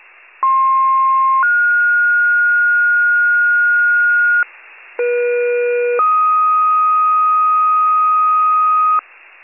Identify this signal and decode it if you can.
Two-tone page, heard as a one-second tone, then a three-second tone, about 1060 then 1430 Hz, about 480 then 1250 Hz, with noise. It is heard from 0.4 to 9.0 s.